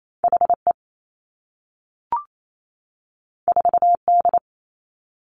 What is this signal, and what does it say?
Morse code, heard as a keyed tone, about 28 wpm, 700 Hz. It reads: H E 4 D
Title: Two Challenging Letter-Number Combinations 28wpm